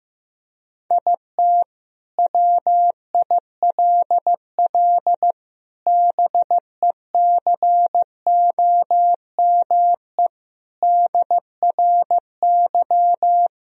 Morse code, heard as a keyed tone, about 15 wpm, 700 Hz. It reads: IT WILL BECOME DRY
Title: Sentences from Top 1000 Words - Review Part 3 - 15wpm